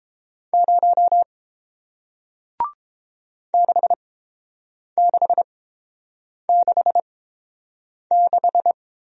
Morse code, heard as a keyed tone, about 30 wpm, spaced as for 7 wpm, 700 Hz. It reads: 0 6666